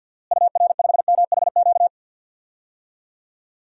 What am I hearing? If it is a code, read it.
Morse code, heard as a keyed tone, about 47 wpm, 700 Hz. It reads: UR5MHQ
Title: DXCC-100 Call Signs - 6 Characters - Repeated 3x - 50wpm